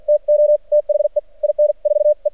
Morse code, heard as a keyed tone, about 53 wpm, spaced as for 45 wpm, 600 Hz. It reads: TO THE IN 4 E
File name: archived / CRSnet